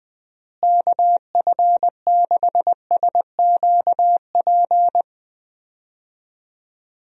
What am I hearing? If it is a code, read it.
Morse code, heard as a keyed tone, about 20 wpm, 700 Hz. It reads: KF6SQP
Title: USA Call Signs - 6 Characters - Repeated 2x - 20wpm